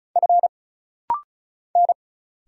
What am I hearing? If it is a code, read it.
Morse code, heard as a keyed tone, about 36 wpm, 700 Hz. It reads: F N
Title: Single Letter-Number Alt - Rapid-Fire 35wpm